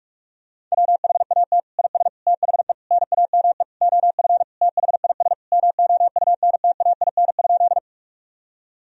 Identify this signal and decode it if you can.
Morse code, heard as a keyed tone, about 41 wpm, 700 Hz. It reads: WHAT IS THE NAME OF THIS MOUNTAIN?
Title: Sentences from Top 1000 Words - Review Part 1 - 45wpm